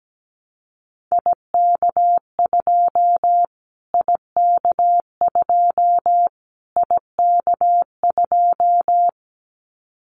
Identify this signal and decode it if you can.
Morse code, heard as a keyed tone, about 17 wpm, 700 Hz. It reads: IK2 IK2 IK2
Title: Callsign Prefixes for Top 25 DXCC - Repeated 3x - 17wpm